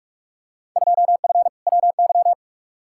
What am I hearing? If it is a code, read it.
Morse code, heard as a keyed tone, about 47 wpm, 700 Hz. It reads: 2F WY
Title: Field Day Exchanges Encore - 45wpm